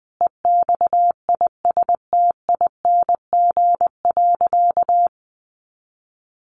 Morse code, heard as a keyed tone, about 19 wpm, 700 Hz. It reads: EXISTING.
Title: Sentences from Top 1500 Words 20wpm